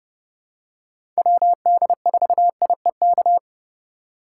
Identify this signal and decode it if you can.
Morse code, heard as a keyed tone, about 30 wpm, 700 Hz. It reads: WD4IEK
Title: USA Call Signs - 4x Character Spacing - 30wpm